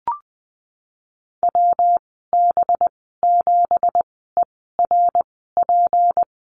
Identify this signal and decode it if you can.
Morse code, heard as a keyed tone, about 20 wpm, spaced as for 15 wpm, 700 Hz. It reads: WB7ERP
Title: 186 Lesson 30g - Letter b - Call Signs 20wpm